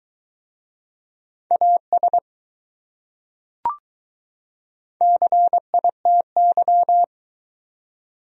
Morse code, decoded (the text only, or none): AS CITY